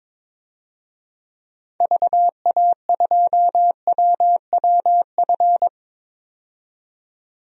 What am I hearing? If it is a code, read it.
Morse code, heard as a keyed tone, about 21 wpm, 700 Hz. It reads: VA2WWF
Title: DXCC-100 Call Signs - 6 Characters - 22wpm